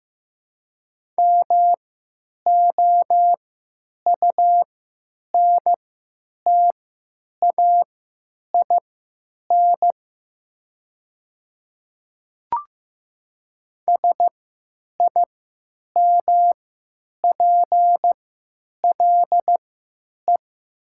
Morse code, heard as a keyed tone, about 15 wpm, spaced as for 9 wpm, 700 Hz. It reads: MOUNTAIN SIMPLE
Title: 3x Character Spacing - Top 500 Words - 15wpm